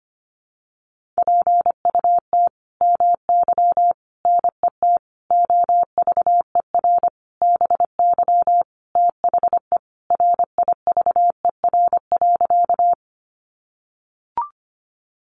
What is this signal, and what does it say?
Morse code, heard as a keyed tone, about 25 wpm, 700 Hz. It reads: PUT MY NET OVER BY THE RIVER.
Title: Sentences from Top 1400 Words 25wpm